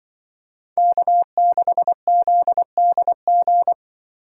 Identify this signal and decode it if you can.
Morse code, heard as a keyed tone, about 24 wpm, 700 Hz. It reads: K6ZDG
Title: Speed Racing - USA 1x3 Call Signs 20wpm